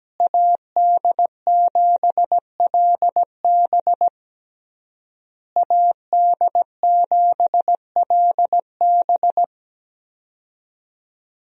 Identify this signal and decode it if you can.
Morse code, heard as a keyed tone, about 17 wpm, 700 Hz. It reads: AD7LB AD7LB
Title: USA Call Signs - 5 Characters - Repeated 3x - 17wpm